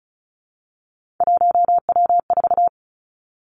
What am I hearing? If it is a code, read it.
Morse code, heard as a keyed tone, about 36 wpm, 700 Hz. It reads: 1W4
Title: Three Challenging Letter-Number Combinations 35wpm